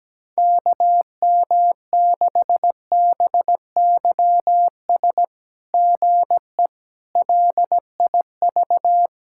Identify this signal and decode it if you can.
Morse code, heard as a keyed tone, about 17 wpm, 700 Hz. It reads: KM6BYS GE LIV
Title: SOTA QSO Practice 17wpm